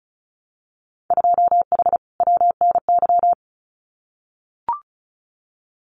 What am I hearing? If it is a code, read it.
Morse code, heard as a keyed tone, about 34 wpm, 700 Hz. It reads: 2H WNY
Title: Winter Field Day Exchanges Encore 35wpm